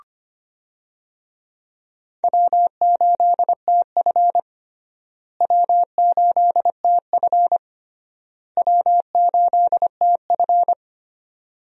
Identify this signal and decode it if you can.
Morse code, heard as a keyed tone, about 25 wpm, 700 Hz. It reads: W8TF W8TF W8TF
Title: USA Call Signs - 4 Characters - Repeated 3x - 25wpm